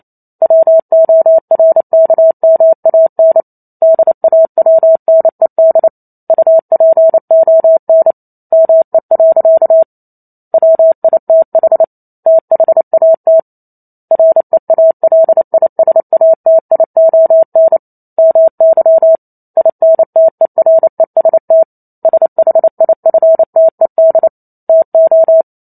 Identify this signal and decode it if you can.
Morse code, heard as a keyed tone, about 28 wpm, 650 Hz. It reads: WORKMAN DAWNED UPON ME. WITH THAT REALISATION MY INTEREST SHIFTED TO